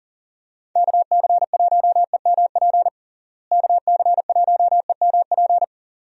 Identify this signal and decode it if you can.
Morse code, heard as a keyed tone, about 40 wpm, 700 Hz. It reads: KC1EMP KC1EMP
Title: USA Call Signs - 6 Characters - Repeated 2x - 40wpm